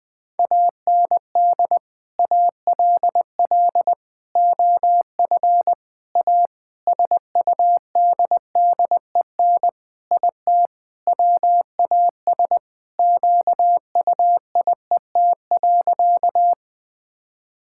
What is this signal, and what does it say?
Morse code, heard as a keyed tone, about 21 wpm, 700 Hz. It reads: AND ALL OF A SUDDEN IT WAS QUIET.
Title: Sentences with 5-Word Phrases - 20wpm